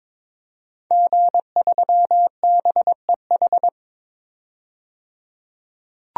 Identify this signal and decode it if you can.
Morse code, heard as a keyed tone, about 21 wpm, 700 Hz. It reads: G3BEH
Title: DXCC-100 Call Signs - 5 Characters - 22wpm